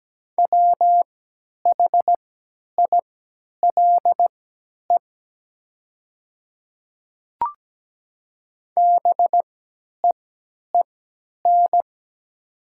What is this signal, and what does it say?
Morse code, heard as a keyed tone, about 17 wpm, spaced as for 10 wpm, 700 Hz. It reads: WHILE BEEN
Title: 3x Character Spacing - Top 500 Words - 17wpm